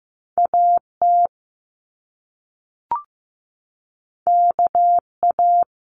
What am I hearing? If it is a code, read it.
Morse code, heard as a keyed tone, about 15 wpm, 700 Hz. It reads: AT KA
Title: Top Two Letter Combinations in Words 15wpm